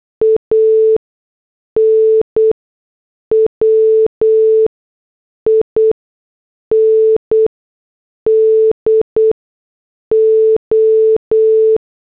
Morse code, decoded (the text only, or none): AN WINDO